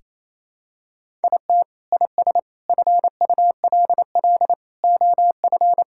Morse code, heard as a keyed tone, about 28 wpm, 700 Hz. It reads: IT IS FULL OF
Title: Sentences from Top 1000 Words - Review Part 5 - 28wpm